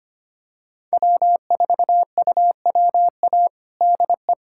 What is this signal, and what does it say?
Morse code, heard as a keyed tone, about 25 wpm, 700 Hz. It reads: W4UWA DE
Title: USA Callsign DE Callsign - 5 Characters - 25wpm